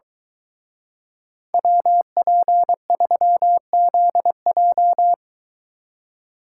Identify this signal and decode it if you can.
Morse code, heard as a keyed tone, about 24 wpm, 700 Hz. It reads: WP3ZJ